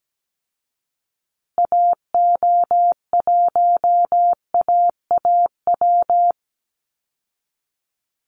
Morse code, heard as a keyed tone, about 17 wpm, 700 Hz. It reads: AO1AAW
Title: Call Signs Repeated 3x - 17wpm